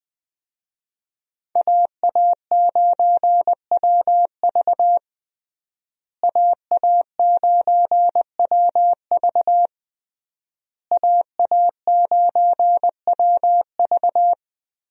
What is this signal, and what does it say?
Morse code, heard as a keyed tone, about 21 wpm, 700 Hz. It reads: AA9WV AA9WV AA9WV
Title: USA Call Signs - 5 Characters - Repeated 3x - 20wpm